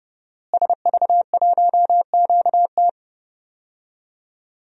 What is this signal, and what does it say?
Morse code, heard as a keyed tone, about 31 wpm, 700 Hz. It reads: SV1QT
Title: DXCC-100 Call Signs - 5 Characters - Repeated 3x - 30wpm